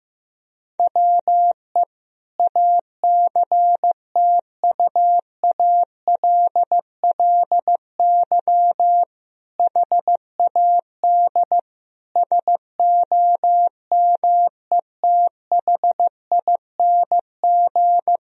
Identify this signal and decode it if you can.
Morse code, heard as a keyed tone, about 15 wpm, 700 Hz. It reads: WE ACTUALLY HAD SOMETHING
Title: Sentences from Top 1000 Words - Review Part 4 - 15wpm